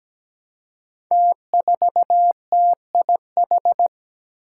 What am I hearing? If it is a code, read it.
Morse code, heard as a keyed tone, about 17 wpm, 700 Hz. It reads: T4TIH
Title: Five Challenging Letter-Number Combinations 17wpm